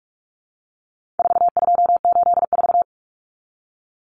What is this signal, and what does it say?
Morse code, heard as a keyed tone, about 44 wpm, 700 Hz. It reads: 4284
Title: Common Cut Numbers - 4-Digit 45wpm